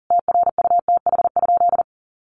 Morse code, heard as a keyed tone, about 40 wpm, 700 Hz. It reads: TRUTH?
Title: Sentences from Top 1000 Words 40wpm